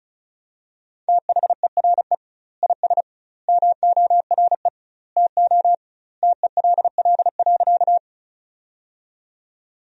Morse code, heard as a keyed tone, about 36 wpm, 700 Hz. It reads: THERE IS MORE TO TELL.
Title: 2x Word-Spacing - Sentences from Top 200 Words - 35wpm